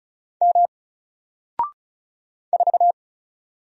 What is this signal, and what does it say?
Morse code, heard as a keyed tone, about 36 wpm, 700 Hz. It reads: M 4
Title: Single Character (Letter:Number:Punctuation:Prosign) Alt 35wpm